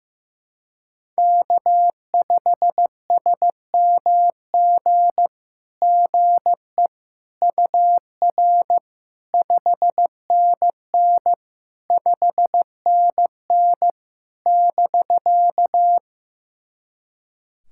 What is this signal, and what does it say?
Morse code, heard as a keyed tone, about 15 wpm, 700 Hz. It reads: K5SMG GE UR 5NN 5NN <BK>
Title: SOTA QSO Practice 15wpm